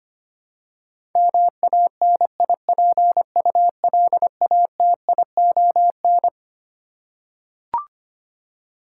Morse code, heard as a keyed tone, about 25 wpm, 700 Hz. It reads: MANIPULATION
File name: Twelve Letter Words - 25wpm